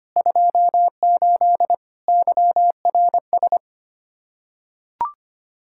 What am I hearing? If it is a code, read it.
Morse code, heard as a keyed tone, about 25 wpm, 700 Hz. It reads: 28 YRS
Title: Practice QSO - Exchange 3 - More Info - 25wpm